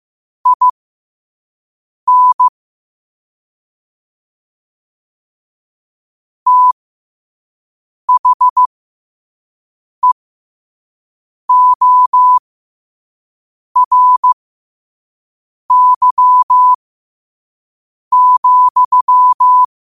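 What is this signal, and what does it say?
Morse code, heard as a keyed tone, about 15 wpm, spaced as for 5 wpm, 1000 Hz. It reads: IN THEORY,
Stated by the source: Morse quote of the day at 5 wpm.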